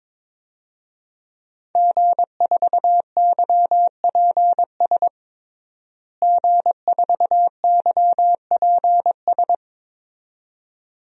Morse code, heard as a keyed tone, about 22 wpm, 700 Hz. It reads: G4YPS G4YPS
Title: DXCC-100 Call Signs - 5 Characters - Repeated 2x - 22wpm